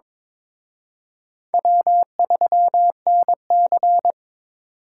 Morse code, heard as a keyed tone, about 26 wpm, 700 Hz. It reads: W3NC
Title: Speed Racing - USA Call Signs 22wpm